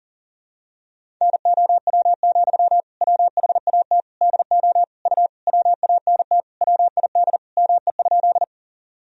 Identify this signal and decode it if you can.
Morse code, heard as a keyed tone, about 40 wpm, 700 Hz. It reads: NOW, WHAT DO U WANT WID ME?
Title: Sentences from Top 200 Words and Abbreviations 40wpm